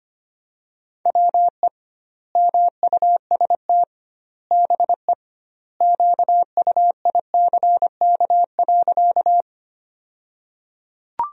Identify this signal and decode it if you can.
Morse code, heard as a keyed tone, about 25 wpm, 700 Hz. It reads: WE MUST BE QUICK.
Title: 2x Word-Spacing - Sentences from Top 500 Words - 25wpm